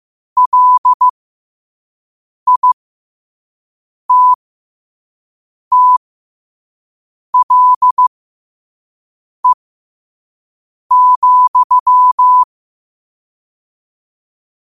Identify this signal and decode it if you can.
Morse code, heard as a keyed tone, about 15 wpm, spaced as for 5 wpm, 1000 Hz. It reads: LITTLE,
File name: qotdfiles / 5wpm